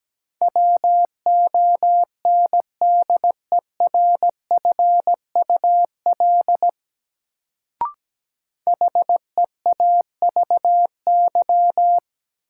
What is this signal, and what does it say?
Morse code, heard as a keyed tone, about 17 wpm, 700 Hz. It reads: WONDERFUL HEAVY
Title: Sentences from Top 1100 Words - New Word Drill Alt - 17wpm